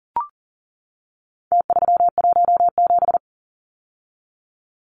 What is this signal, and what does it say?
Morse code, heard as a keyed tone, about 40 wpm, 700 Hz. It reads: T317
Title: Numbers - Contest Style - 4-Digit 40wpm